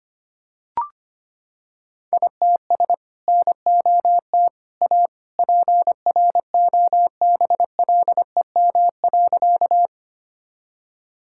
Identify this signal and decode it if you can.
Morse code, heard as a keyed tone, about 25 wpm, 700 Hz. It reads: ITS NOT A PROBLEM.
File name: practice / 25wpm / Sentences with 2-Word Phrases - 25wpm